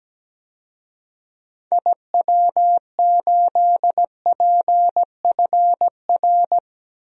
Morse code, heard as a keyed tone, about 17 wpm, 700 Hz. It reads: IW8PFR
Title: DXCC-100 Call Signs - 6 Characters - Repeated 3x - 17wpm